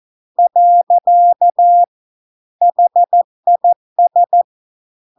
Morse code, heard as a keyed tone, about 14 wpm, 700 Hz. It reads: . HIS